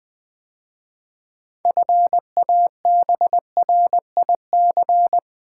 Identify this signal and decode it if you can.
Morse code, heard as a keyed tone, about 20 wpm, 700 Hz. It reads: FABRIC
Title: 6 Letter Words - First 3 Letters Given - 20wpm